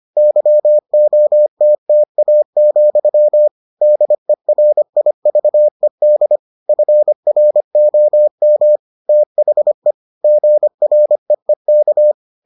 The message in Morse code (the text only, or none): YOTTA, DERIVED FROM THE GREEK